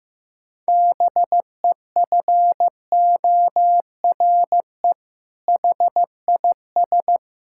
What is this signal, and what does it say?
Morse code, heard as a keyed tone, about 15 wpm, 700 Hz. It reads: BEFORE HIS
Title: Sentences from Top 1000 Words - Review Part 5 - 15wpm